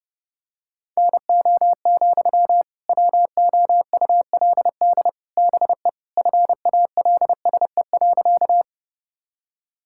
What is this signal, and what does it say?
Morse code, heard as a keyed tone, about 30 wpm, 700 Hz. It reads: NO, WOULD BE FALSE.
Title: Sentences from Top 1600 Words 30wpm